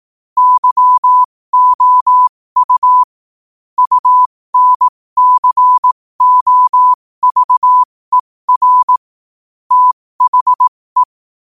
Morse code, heard as a keyed tone, about 18 wpm, spaced as for 15 wpm, 1000 Hz. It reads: YOU UNCOVER THE
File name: qotdfiles / 15wpm